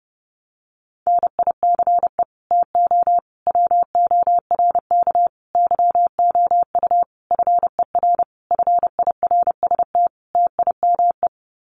Morse code, heard as a keyed tone, about 30 wpm, 700 Hz. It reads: NICE TO WORK YOU FER FIRST TIME
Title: Common QSO Phrases 30wpm